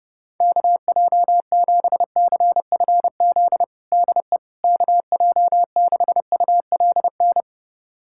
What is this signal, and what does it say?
Morse code, heard as a keyed tone, about 30 wpm, 700 Hz. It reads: KJ7CFZ DE KJ6ULN
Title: USA Callsign DE Callsign - 6 Characters - 30wpm